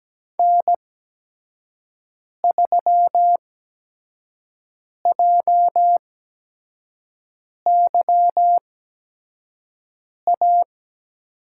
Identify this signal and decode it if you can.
Morse code, heard as a keyed tone, about 17 wpm, spaced as for 5 wpm, 700 Hz. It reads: N3JYA
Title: USA Call Signs - 5 Characters - 8x Character Spacing - 17wpm